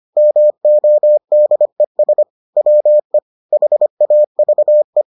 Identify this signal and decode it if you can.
Morse code, heard as a keyed tone, about 25 wpm, 600 Hz. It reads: MODES WE HAVE